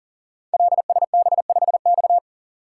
Morse code, heard as a keyed tone, about 38 wpm, 700 Hz. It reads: LSB5X
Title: Code Groups - 5 Letter:Numbers - 40wpm